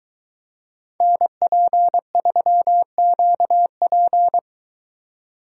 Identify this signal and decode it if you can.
Morse code, heard as a keyed tone, about 22 wpm, 700 Hz. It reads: NP3QP